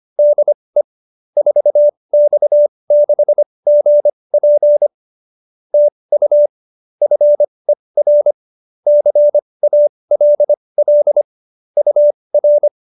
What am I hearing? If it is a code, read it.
Morse code, heard as a keyed tone, about 25 wpm, spaced as for 20 wpm, 590 Hz. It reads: DE 4X6GP TU FER CALL UR
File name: QSO310 20